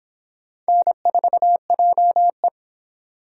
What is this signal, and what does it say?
Morse code, heard as a keyed tone, about 24 wpm, 700 Hz. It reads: N4JE